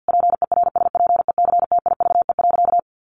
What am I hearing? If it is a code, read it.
Morse code, heard as a keyed tone, about 50 wpm, 700 Hz. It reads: PERSPECTIVE.
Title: Sentences from Top 1200 Words 50wpm